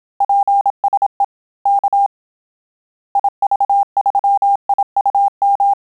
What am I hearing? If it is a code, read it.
Morse code, heard as a keyed tone, about 27 wpm, 800 Hz. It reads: PSE K IV3IUM
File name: Telegraphy-group / QSO CW 25 PPM